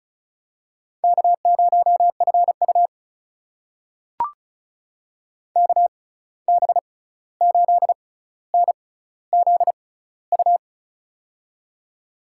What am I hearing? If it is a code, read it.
Morse code, heard as a keyed tone, about 36 wpm, spaced as for 16 wpm, 700 Hz. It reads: K0FU KB8NZU
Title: USA Call Signs - 6x Character Spacing - 35wpm